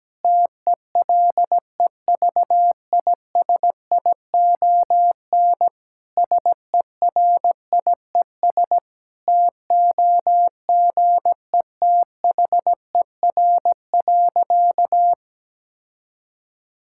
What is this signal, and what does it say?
Morse code, heard as a keyed tone, about 17 wpm, 700 Hz. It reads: TELEVISION SERIES TOGETHER.
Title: Sentences from Top 1000 Words - Review Part 5 - 17wpm